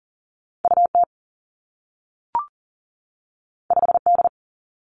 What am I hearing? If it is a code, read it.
Morse code, heard as a keyed tone, about 40 wpm, 700 Hz. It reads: UT 5D
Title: Two Challenging Letter-Number Combinations 40wpm